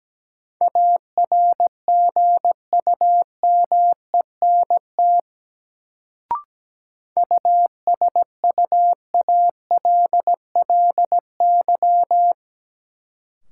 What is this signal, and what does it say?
Morse code, heard as a keyed tone, about 17 wpm, 700 Hz. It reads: ARGUMENT USUALLY
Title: Sentences from Top 1400 Words - New Word Drill Alt - 17wpm